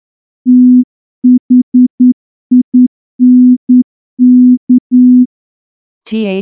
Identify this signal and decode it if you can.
Morse code, heard as a keyed tone, about 10 wpm, 250 Hz. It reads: THINK
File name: morse-training-10WPM